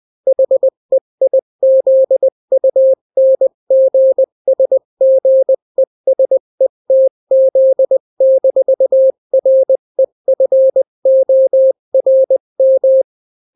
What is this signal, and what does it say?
Morse code, heard as a keyed tone, about 20 wpm, 530 Hz. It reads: HEIZUNGSGESETZ-REFORM
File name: daily / 18wpm